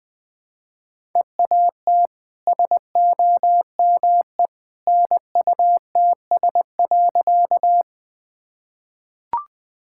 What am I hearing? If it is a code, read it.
Morse code, heard as a keyed tone, about 21 wpm, 700 Hz. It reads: EAT SOME NUTS.
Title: Easy Sentences with Hints - 20wpm